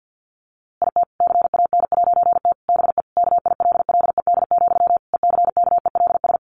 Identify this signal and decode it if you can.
Morse code, heard as a keyed tone, about 50 wpm, 700 Hz. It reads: IT CAN'T BE KILLED, ECKELS
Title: A Sound Of Thunder 50wpm